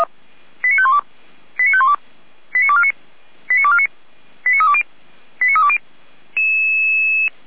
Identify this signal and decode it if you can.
Five-tone selective call (ZVEI), heard as quick runs of five tones, tones of about 70 ms. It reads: 89512 89512 89249 89249 89320 89320
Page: audio.f85360.feuerwehr-orlamuende.org